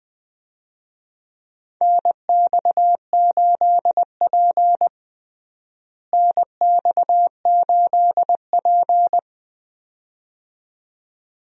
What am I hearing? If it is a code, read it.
Morse code, heard as a keyed tone, about 20 wpm, 700 Hz. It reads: NX8P NX8P
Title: USA Call Signs - 4 Characters - Repeated 2x - 20wpm